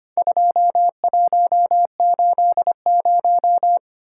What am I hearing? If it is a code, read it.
Morse code, heard as a keyed tone, about 25 wpm, 700 Hz. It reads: 2180
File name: practice / 25wpm / Four Letter-Number Combinations - 25wpm